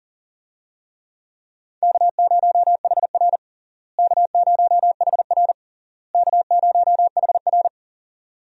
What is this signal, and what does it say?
Morse code, heard as a keyed tone, about 40 wpm, 700 Hz. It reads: K0HR K0HR K0HR
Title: USA Call Signs - 4 Characters - Repeated 3x - 40wpm